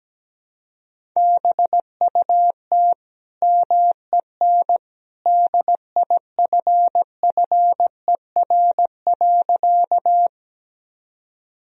Morse code, heard as a keyed tone, about 17 wpm, 700 Hz. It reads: BUT MEN DIFFER.
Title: Sentences from Top 1000 Words - Review Part 1 - 17wpm